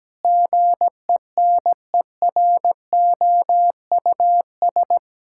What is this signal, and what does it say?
Morse code, heard as a keyed tone, about 17 wpm, 700 Hz. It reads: GENEROUS
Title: Pangram Challenge v2 - 17wpm